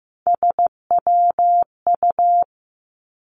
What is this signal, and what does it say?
Morse code, heard as a keyed tone, about 15 wpm, 700 Hz. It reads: SWU
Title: Three Challenging Letter-Number Combinations 15wpm